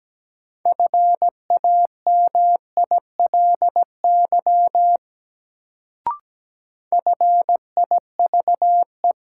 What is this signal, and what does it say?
Morse code, heard as a keyed tone, about 17 wpm, 700 Hz. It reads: FAMILY FIVE
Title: Sentences from Top 400 Words - New Word Drill - 17wpm